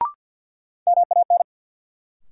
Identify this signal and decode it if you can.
Morse code, heard as a keyed tone, about 53 wpm, 700 Hz. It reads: MAN
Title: Sentences from Top 200 Words - New Word Drill - 50wpm